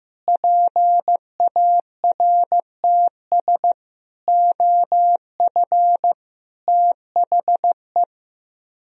Morse code, heard as a keyed tone, about 15 wpm, 700 Hz. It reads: PARTS OF THE
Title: Speed-Racing - Sets of 5 Words 15wpm